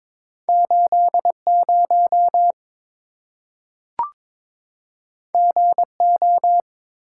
Morse code, heard as a keyed tone, about 23 wpm, 700 Hz. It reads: 80 GO
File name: practice / 22wpm / Two Letter-Number Combinations - 22wpm